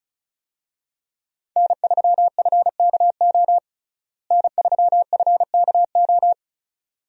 Morse code, heard as a keyed tone, about 36 wpm, 700 Hz. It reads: N3FKO N3FKO
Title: USA Call Signs - 5 Characters - Repeated 2x - 35wpm